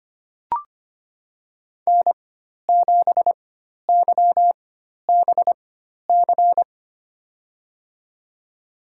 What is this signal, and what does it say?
Morse code, heard as a keyed tone, about 25 wpm, spaced as for 12 wpm, 700 Hz. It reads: N7YBC
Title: USA Call Signs - 5 Characters - 4x Character Spacing - 25wpm